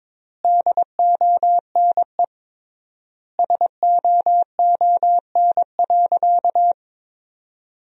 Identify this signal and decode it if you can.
Morse code, heard as a keyed tone, about 22 wpm, 700 Hz. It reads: DONE SOON.
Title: 3x Word-Spacing - Sentences from Top 500 Words - 22wpm